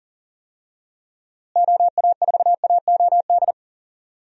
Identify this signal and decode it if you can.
Morse code, heard as a keyed tone, about 40 wpm, 700 Hz. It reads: OA4AOD
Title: USA Call Signs - 6 Characters - Repeated 2x - 40wpm